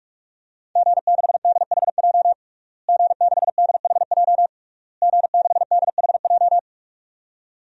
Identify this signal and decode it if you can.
Morse code, heard as a keyed tone, about 43 wpm, 700 Hz. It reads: G6DHJ G6DHJ G6DHJ
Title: DXCC-100 Call Signs - 5 Characters - Repeated 3x - 45wpm